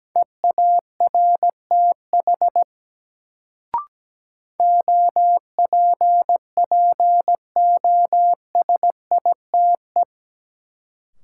Morse code, heard as a keyed tone, about 17 wpm, 700 Hz. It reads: EARTH OPPOSITE
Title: Sets of 5 Words Encore - New Word Drill - 17wpm